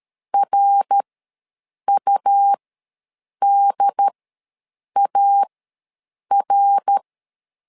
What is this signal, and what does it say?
Morse code, heard as a keyed tone, about 13 wpm, spaced as for 7 wpm, 800 Hz. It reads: RUDAR
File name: LEZIONI / 3 MWRO / 3 MWR ws 12wpm 1809